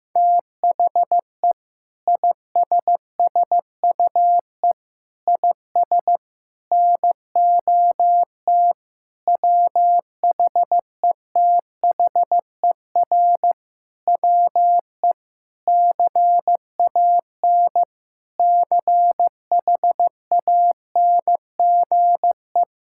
Morse code, heard as a keyed tone, about 16 wpm, 700 Hz. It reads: THE ISSUE IS NOT WHETHER WE CAN CHANGE
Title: Sentences from Top 1000 Words - Review Part 6 - 15wpm